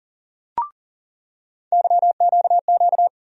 Morse code, heard as a keyed tone, about 40 wpm, 700 Hz. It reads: YQQ
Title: USA Call Sign Postfixes - 3 Letters 40wpm